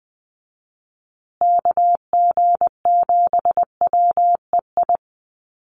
Morse code, heard as a keyed tone, about 20 wpm, 700 Hz. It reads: KG7WEI
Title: USA Call Signs Repeated 2x 20wpm